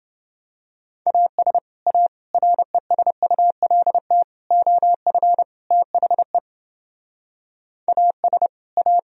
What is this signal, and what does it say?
Morse code, heard as a keyed tone, about 28 wpm, 700 Hz. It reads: AS A RESULT OF THE AS A